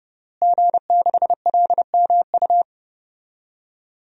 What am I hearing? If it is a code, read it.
Morse code, heard as a keyed tone, about 32 wpm, 700 Hz. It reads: G6LMU
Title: DXCC-100 Call Signs - 5 Characters - Repeated 3x - 30wpm